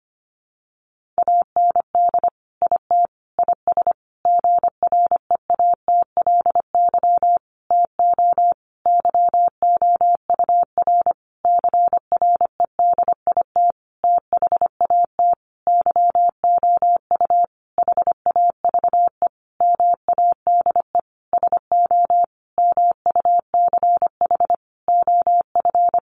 Morse code, heard as a keyed tone, about 25 wpm, 700 Hz. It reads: AND IT IS GREATLY TO YOUR CREDIT THAT YOU HAVE MADE SO MUCH OF
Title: Sentences from Top 1400 Words 25wpm